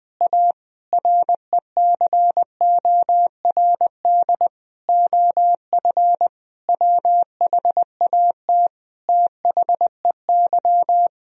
Morse code, heard as a keyed tone, about 20 wpm, 700 Hz. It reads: A RECORD OF WHAT THEY
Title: Sentences from Top 1000 Words - Review Part 3 - 20wpm